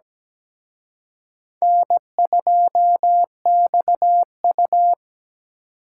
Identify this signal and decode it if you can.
Morse code, heard as a keyed tone, about 20 wpm, 700 Hz. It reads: N2XU
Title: Speed Racing - USA 1x2 Call Signs 17wpm